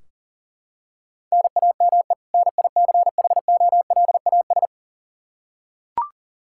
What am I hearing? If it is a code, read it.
Morse code, heard as a keyed tone, about 40 wpm, 700 Hz. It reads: NAME NICHOLAS
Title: K1USN SST QSO Practice 40wpm